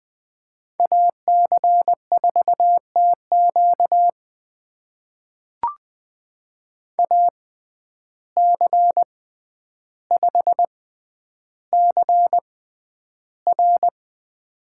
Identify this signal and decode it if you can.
Morse code, heard as a keyed tone, about 20 wpm, spaced as for 10 wpm, 700 Hz. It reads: AC4TQ AC5CR
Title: USA Call Signs - 5 Characters - 6x Character Spacing - 20wpm